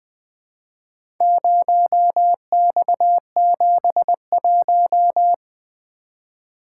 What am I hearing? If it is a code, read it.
Morse code, heard as a keyed tone, about 20 wpm, 700 Hz. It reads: 0X71
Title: Four Letter-Number Combinations - 20wpm